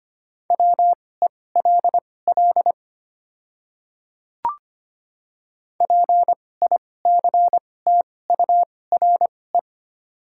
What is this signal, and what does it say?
Morse code, heard as a keyed tone, about 25 wpm, spaced as for 18 wpm, 700 Hz. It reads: WELL PICTURE
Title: 2x Character Spacing - Top 200 Words - 25wpm